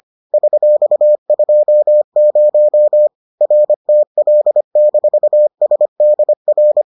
Morse code, heard as a keyed tone, about 25 wpm, 600 Hz. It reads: $20 RTL-SDR